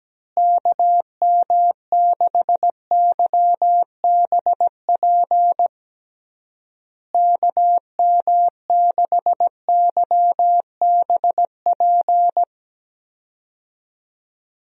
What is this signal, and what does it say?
Morse code, heard as a keyed tone, about 17 wpm, 700 Hz. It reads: KM6YBP KM6YBP
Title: USA Call Signs - 6 Characters - Repeated 3x - 17wpm